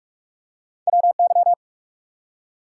Morse code, heard as a keyed tone, about 44 wpm, 700 Hz. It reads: WY
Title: US States and Canadian Provinces 45wpm